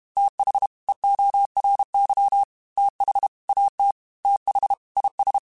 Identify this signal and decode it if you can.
Morse code, heard as a keyed tone, about 32 wpm, 800 Hz. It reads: TH EORY TH AT TH IS